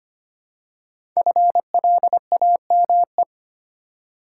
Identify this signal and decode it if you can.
Morse code, heard as a keyed tone, about 25 wpm, 700 Hz. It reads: FLAME
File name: 5 Letter Words - First 3 Letters Given - 25wpm